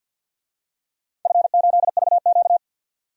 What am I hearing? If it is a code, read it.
Morse code, heard as a keyed tone, about 51 wpm, 700 Hz. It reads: F8VQ
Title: DXCC-100 Call Signs - 4 Characters - 50wpm